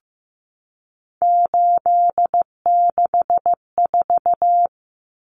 Morse code, heard as a keyed tone, about 15 wpm, 700 Hz. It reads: 864
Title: Common Cut Numbers - 3-Digit 15wpm